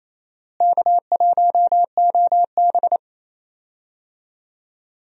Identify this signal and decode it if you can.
Morse code, heard as a keyed tone, about 28 wpm, 700 Hz. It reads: K1OB
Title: USA Call Signs - 4 Characters - Repeated 2x - 28wpm